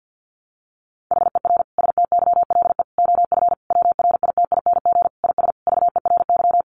Morse code, heard as a keyed tone, about 50 wpm, 700 Hz. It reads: HER STYLE OF WRITING IS VERY
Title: Sentences from Top 800 Words 50wpm